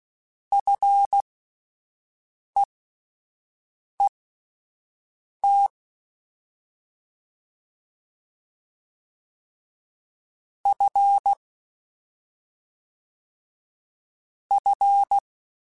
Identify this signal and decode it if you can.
Morse code, heard as a keyed tone, about 16 wpm, spaced as for 5 wpm, 800 Hz. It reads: FEET F F